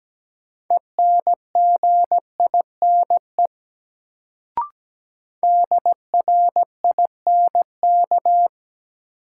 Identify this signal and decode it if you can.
Morse code, heard as a keyed tone, about 17 wpm, 700 Hz. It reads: ENGINE DRINK
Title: Sentences from Top 1500 Words - New Word Drill Alt - 17wpm